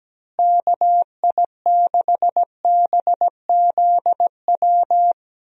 Morse code, heard as a keyed tone, about 17 wpm, 700 Hz. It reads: KI6BZW
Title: USA Call Signs - 6 Characters - Repeated 3x - 17wpm